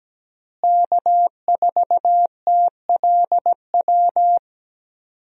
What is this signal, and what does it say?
Morse code, heard as a keyed tone, about 17 wpm, 700 Hz. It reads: K4TLW
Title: USA Call Signs - 5 Characters - Repeated 2x - 17wpm